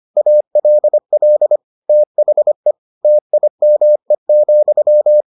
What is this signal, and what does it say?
Morse code, heard as a keyed tone, about 25 wpm, 600 Hz. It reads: ALL THE TIME,